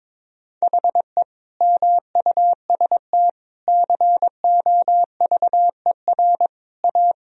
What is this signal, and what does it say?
Morse code, heard as a keyed tone, about 22 wpm, 700 Hz. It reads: HE MUST COVER A
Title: Sentences from Top 500 Words and Abbreviations 22wpm